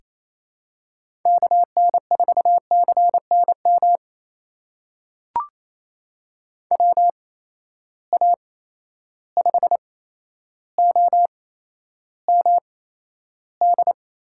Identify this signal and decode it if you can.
Morse code, heard as a keyed tone, about 28 wpm, spaced as for 12 wpm, 700 Hz. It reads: KN4CNM WA5OMD
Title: USA Call Signs - 6 Characters - 8x Character Spacing - 28wpm